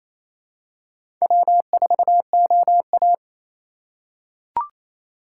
Morse code, heard as a keyed tone, about 28 wpm, 700 Hz. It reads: W4OA
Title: USA Call Signs - 4x Character Spacing - 28wpm